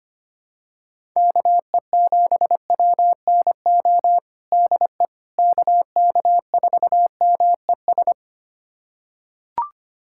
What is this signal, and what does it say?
Morse code, heard as a keyed tone, about 25 wpm, 700 Hz. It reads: KE7WNO DE KK4MES
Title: USA Callsign DE Callsign - 6 Characters - 25wpm